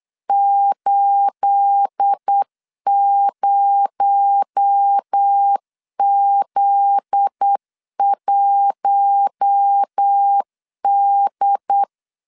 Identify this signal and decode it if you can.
Morse code, heard as a keyed tone, about 8 wpm, 800 Hz. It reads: 80Z1D
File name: CW / DET / 1 9 1500C S B 8wpm 5624